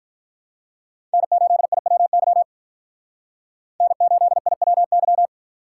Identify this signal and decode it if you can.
Morse code, heard as a keyed tone, about 46 wpm, 700 Hz. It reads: N8IWY N8IWY